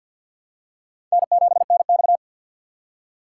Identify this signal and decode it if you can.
Morse code, heard as a keyed tone, about 53 wpm, 700 Hz. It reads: N7NX
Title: USA Call Signs - 3x Character Spacing - 50wpm